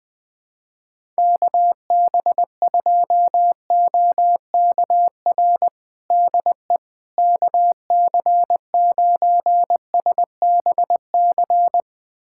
Morse code, heard as a keyed tone, about 20 wpm, 700 Hz. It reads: KB2OKR DE KC9SBC
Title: USA Callsign DE Callsign - 6 Characters - 20wpm